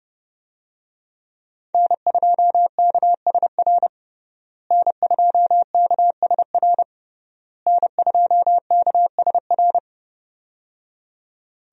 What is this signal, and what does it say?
Morse code, heard as a keyed tone, about 30 wpm, 700 Hz. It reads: N2KSR N2KSR N2KSR
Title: USA Call Signs - 5 Characters - Repeated 3x - 30wpm